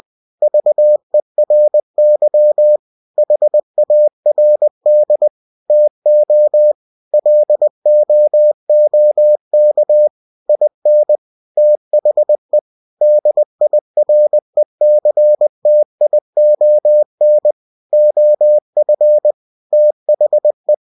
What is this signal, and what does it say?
Morse code, heard as a keyed tone, about 20 wpm, 600 Hz. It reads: VERY HARD TO LOOK IN THE DIRECTION OF THE
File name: cw-books / winnie-the-pooh / winnie-20wpm-page-0015